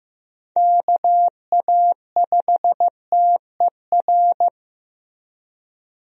Morse code, heard as a keyed tone, about 15 wpm, 700 Hz. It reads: KA5TER
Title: USA Call Signs - 6 Characters - Repeated 3x - 15wpm